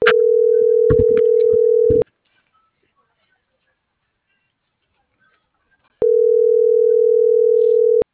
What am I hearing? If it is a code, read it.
Ringback tone (North America), 440 + 480 Hz.